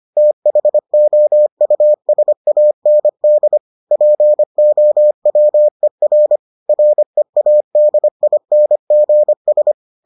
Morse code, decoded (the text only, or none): THOUSAND POWER READINGS